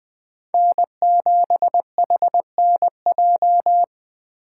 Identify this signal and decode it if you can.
Morse code, heard as a keyed tone, about 20 wpm, 700 Hz. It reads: N7HNJ